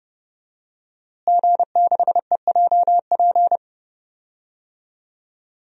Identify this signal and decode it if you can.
Morse code, heard as a keyed tone, about 30 wpm, 700 Hz. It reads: G6EJP
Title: DXCC-100 Call Signs - 5 Characters - Repeated 2x - 30wpm